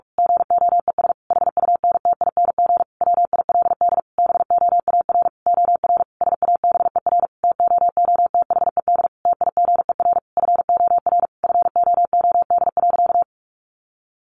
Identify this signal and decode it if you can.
Morse code, heard as a keyed tone, about 44 wpm, 700 Hz. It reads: GOES HUNTING WILD BOAR OR SABER TOOTHED TIGER FOR FOOD.
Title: A Sound Of Thunder 45wpm